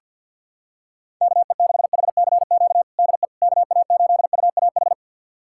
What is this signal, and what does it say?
Morse code, heard as a keyed tone, about 47 wpm, 700 Hz. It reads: KE6HCQ DE KA8UAH
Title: USA Callsign DE Callsign 50wpm